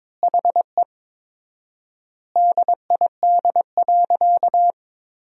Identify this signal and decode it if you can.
Morse code, heard as a keyed tone, about 21 wpm, 700 Hz. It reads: HE DID.
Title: 4x Word-Spacing - Sentences from Top 500 Words - 22wpm